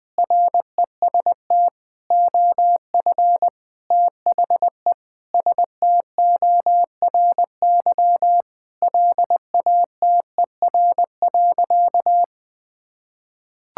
Morse code, heard as a keyed tone, about 21 wpm, 700 Hz. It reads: REST OF THE STORY LATER.
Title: Sentences with 4-Word Phrases - 20wpm